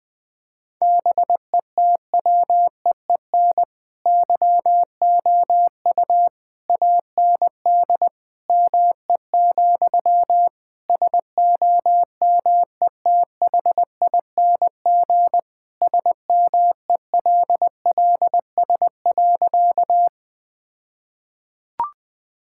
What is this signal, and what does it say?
Morse code, heard as a keyed tone, about 21 wpm, 700 Hz. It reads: BETWEEN YOU AND ME, SOMETHING SMELLS.
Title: All-Age Appropriate Jokes - 20wpm